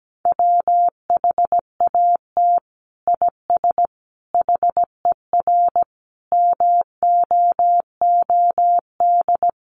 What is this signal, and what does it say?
Morse code, heard as a keyed tone, about 17 wpm, 700 Hz. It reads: WHAT IS HER MOOD
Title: Sentences from Top 1100 Words 17wpm